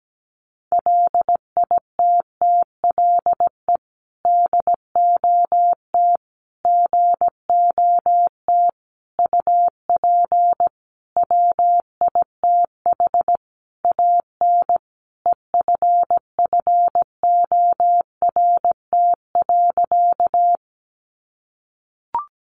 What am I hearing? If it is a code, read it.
Morse code, heard as a keyed tone, about 17 wpm, 700 Hz. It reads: LITTLE DOT GOT UP WITH AN EFFORT.
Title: Sentences from Top 1500 Words 17wpm